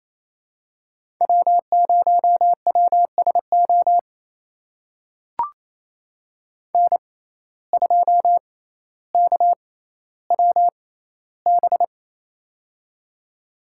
Morse code, heard as a keyed tone, about 29 wpm, spaced as for 15 wpm, 700 Hz. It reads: W0WSO N2KWB
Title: USA Call Signs - 5 Characters - 6x Character Spacing - 28wpm